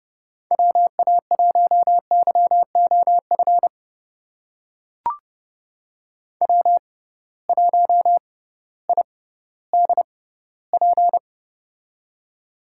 Morse code, heard as a keyed tone, about 31 wpm, spaced as for 16 wpm, 700 Hz. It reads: WA1YOF W1IDP
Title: USA Call Signs - 6x Character Spacing - 30wpm